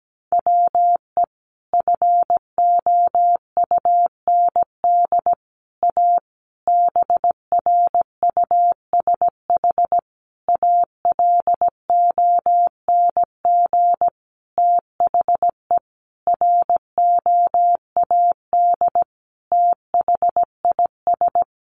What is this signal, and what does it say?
Morse code, heard as a keyed tone, about 17 wpm, 700 Hz. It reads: WE FOUND A BRUSH ALONG THE ROAD THIS
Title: Sentences from Top 1100 Words 17wpm